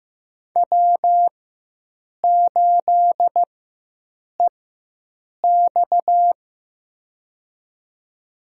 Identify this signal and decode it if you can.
Morse code, heard as a keyed tone, about 15 wpm, spaced as for 7 wpm, 700 Hz. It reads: W8EX
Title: USA Call Signs - 4 Characters - 4x Character Spacing - 15wpm